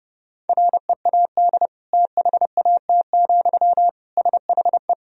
Morse code, heard as a keyed tone, about 30 wpm, 700 Hz. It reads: READ THAT, SHE
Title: Sentences from Top 200 Words and Abbreviations 30wpm